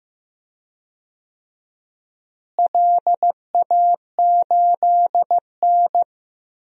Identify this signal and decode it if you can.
Morse code, heard as a keyed tone, about 15 wpm, 700 Hz. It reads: LA8N
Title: DXCC-100 Call Signs - 4 Characters - Repeated 3x - 15wpm